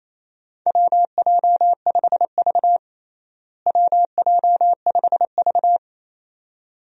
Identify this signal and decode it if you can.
Morse code, heard as a keyed tone, about 28 wpm, 700 Hz. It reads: WJ5V WJ5V
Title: USA Call Signs - 4 Characters - Repeated 2x - 28wpm